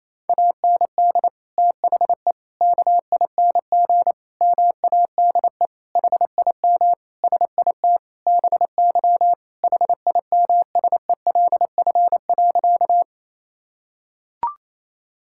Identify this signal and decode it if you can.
Morse code, heard as a keyed tone, about 28 wpm, 700 Hz. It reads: AND THE KING MADE HIM SIT BY HIMSELF.
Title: Sentences from Top 1600 Words 28wpm